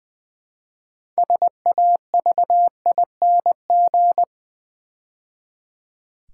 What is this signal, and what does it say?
Morse code, heard as a keyed tone, about 20 wpm, 700 Hz. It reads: SAVING
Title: Build A Word - 3 Letters At A Time - 6-Letter Words - 20wpm